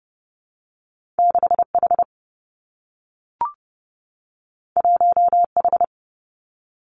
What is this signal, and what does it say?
Morse code, heard as a keyed tone, about 30 wpm, 700 Hz. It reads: 6 H 1 H
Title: Two Challenging Letter-Number Combinations 30wpm